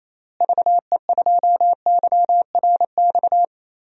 Morse code, heard as a keyed tone, about 28 wpm, 700 Hz. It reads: VE2YRX
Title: DXCC-100 Call Signs - 6 Characters - Repeated 3x - 28wpm